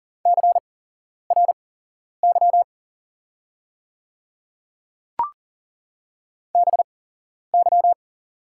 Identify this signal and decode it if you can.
Morse code, heard as a keyed tone, about 40 wpm, spaced as for 11 wpm, 700 Hz. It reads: CRY BY
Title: 8x Character Spacing - Top 500 Words - 40wpm